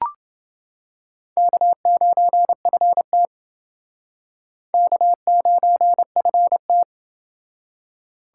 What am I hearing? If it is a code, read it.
Morse code, heard as a keyed tone, about 28 wpm, 700 Hz. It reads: K9FT K9FT